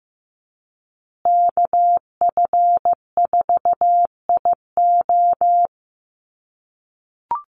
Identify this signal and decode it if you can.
Morse code, heard as a keyed tone, about 15 wpm, 700 Hz. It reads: KF4IO
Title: USA Call Signs Repeated 2x 15wpm